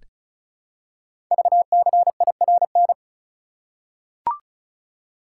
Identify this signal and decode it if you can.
Morse code, heard as a keyed tone, about 36 wpm, 700 Hz. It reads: VCIRN
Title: Code Groups - 5 Letters - 35wpm